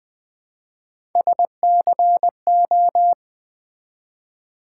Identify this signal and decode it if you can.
Morse code, heard as a keyed tone, about 20 wpm, 700 Hz. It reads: SCO
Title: Build A Word - 1 Letter At A Time - 5-Letter Words - 20wpm